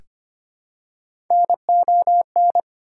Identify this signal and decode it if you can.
Morse code, heard as a keyed tone, about 25 wpm, 700 Hz. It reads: NON
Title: Prefixes and Suffixes 25wpm